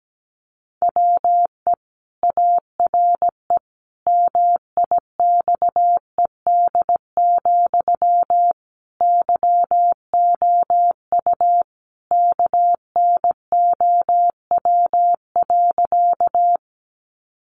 Morse code, heard as a keyed tone, about 17 wpm, 700 Hz. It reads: WE ARE MIXED, YOU KNOW.
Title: Sentences from Top 1500 Words 17wpm